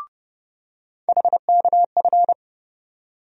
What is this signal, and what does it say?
Morse code, heard as a keyed tone, about 30 wpm, 700 Hz. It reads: HKF
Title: USA Call Sign Postfixes - 3 Letters 30wpm